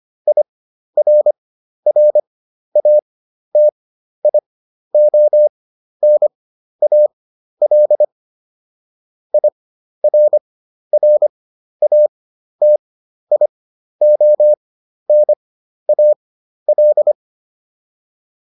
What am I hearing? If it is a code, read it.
Morse code, heard as a keyed tone, about 25 wpm, spaced as for 12 wpm, 600 Hz. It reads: IRRATIONAL IRRATIONAL